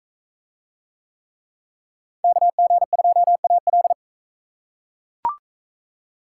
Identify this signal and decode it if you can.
Morse code, heard as a keyed tone, about 42 wpm, 700 Hz. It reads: KG2AL